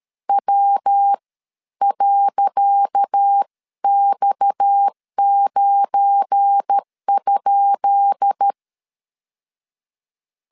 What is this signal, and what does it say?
Morse code, heard as a keyed tone, about 13 wpm, spaced as for 11 wpm, 800 Hz. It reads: W.X9?
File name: LEZIONI / 10 SIGNS / 10C DOT 12wpm 2451